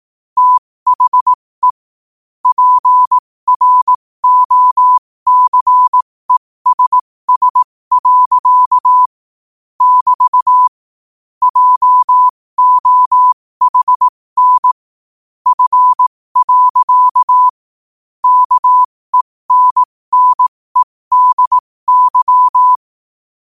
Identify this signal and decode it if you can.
Morse code, heard as a keyed tone, about 18 wpm, spaced as for 15 wpm, 1000 Hz. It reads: THE PROCESS. <BT> JOHN F. KENNEDY
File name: qotdfiles / 15wpm